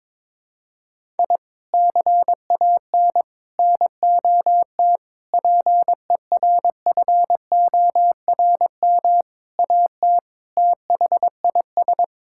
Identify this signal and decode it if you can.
Morse code, heard as a keyed tone, about 22 wpm, 700 Hz. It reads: I CAN NOT PERFORM AT THIS
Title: Sentences from Top 1000 Words - Review Part 5 - 22wpm